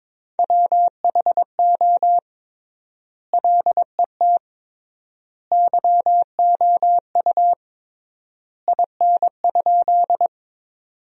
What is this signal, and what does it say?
Morse code, heard as a keyed tone, about 22 wpm, 700 Hz. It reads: WHO LET YOU IN?
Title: 3x Word-Spacing - Sentences from Top 300 Words - 22wpm